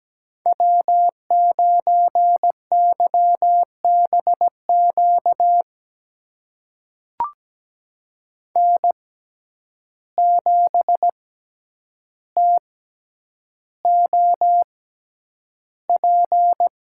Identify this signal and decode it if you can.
Morse code, heard as a keyed tone, about 17 wpm, spaced as for 9 wpm, 700 Hz. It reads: W9YBQ N7TOP
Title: USA Call Signs - 5 Characters - 6x Character Spacing - 17wpm